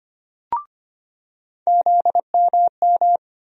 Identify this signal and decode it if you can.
Morse code, heard as a keyed tone, about 25 wpm, 700 Hz. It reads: ZMM
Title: USA Call Sign Postfixes - 3 Letters 25wpm